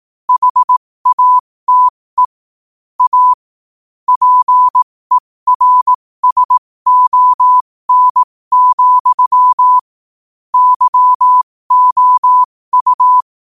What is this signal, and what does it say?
Morse code, heard as a keyed tone, about 18 wpm, spaced as for 15 wpm, 1000 Hz. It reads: HATE A PERSON, YOU
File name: qotdfiles / 15wpm